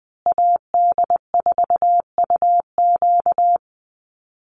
Morse code, heard as a keyed tone, about 20 wpm, 700 Hz. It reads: AD4UQ
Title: USA Call Signs Repeated 3x 20wpm